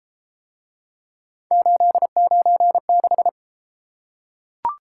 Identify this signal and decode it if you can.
Morse code, heard as a keyed tone, about 31 wpm, 700 Hz. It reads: 896